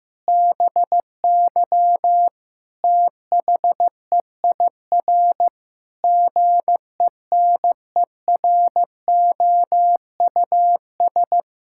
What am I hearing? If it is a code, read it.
Morse code, heard as a keyed tone, about 15 wpm, 700 Hz. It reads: BY THEIR GENEROUS
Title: Pangram Challenge v2 - 15wpm